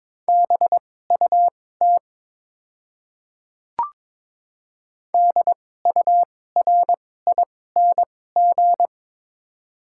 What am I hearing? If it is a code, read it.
Morse code, heard as a keyed tone, about 22 wpm, spaced as for 16 wpm, 700 Hz. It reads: BUT DURING
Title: 2x Character Spacing - Top 500 Words - 22wpm